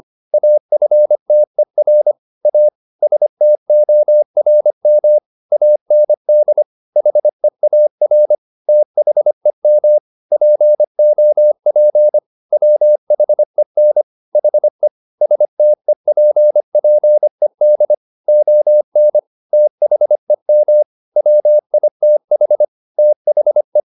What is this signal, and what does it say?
Morse code, heard as a keyed tone, about 25 wpm, 600 Hz. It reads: AFTER A STORM AND HEAR THEM POP WHEN HE STEPPED ON THEM WITH THE
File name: the old man and the sea part 1 25wpm